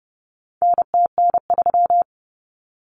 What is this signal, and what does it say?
Morse code, heard as a keyed tone, about 30 wpm, 700 Hz. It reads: NTN3
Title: Common Cut Numbers - 4-Digit 30wpm